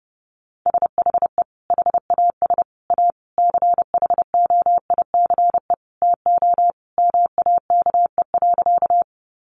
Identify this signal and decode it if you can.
Morse code, heard as a keyed tone, about 30 wpm, 700 Hz. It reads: SHE HAS A CHOICE TO MAKE.
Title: Sentences from Top 900 Words 30wpm